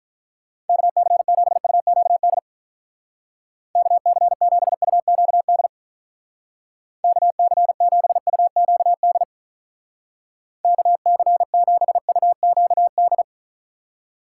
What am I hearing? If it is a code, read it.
Morse code, heard as a keyed tone, about 45 wpm, 700 Hz. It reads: KC7UQD KC7UQD KC7UQD KC7UQD